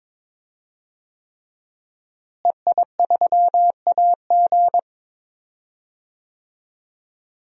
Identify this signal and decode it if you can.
Morse code, heard as a keyed tone, about 21 wpm, 700 Hz. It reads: EI3AG
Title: DXCC-100 Call Signs - 5 Characters - Repeated 3x - 22wpm